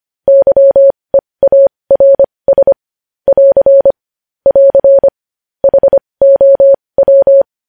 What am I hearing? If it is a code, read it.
Morse code, heard as a keyed tone, about 25 wpm, spaced as for 20 wpm, 550 Hz. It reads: YEARS <AR> <AR> HOW
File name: podcast / News headlines-1x-25x20